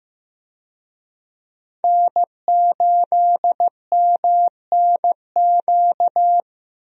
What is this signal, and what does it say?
Morse code, heard as a keyed tone, about 15 wpm, 700 Hz. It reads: N8MNQ
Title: USA Call Signs - 5 Characters - 15wpm